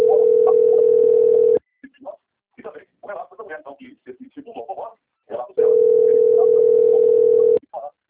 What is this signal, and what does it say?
Ringback tone (North America), 440 + 480 Hz, with speech.
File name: out- - - - - .25415